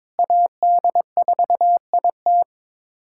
Morse code, heard as a keyed tone, about 22 wpm, 700 Hz. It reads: AD4IT
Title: USA Callsign DE Callsign - 5 Characters - 22wpm